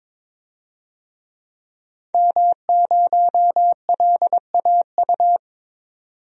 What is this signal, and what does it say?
Morse code, heard as a keyed tone, about 21 wpm, 700 Hz. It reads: M0LAU
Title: DXCC-100 Call Signs - 5 Characters - 22wpm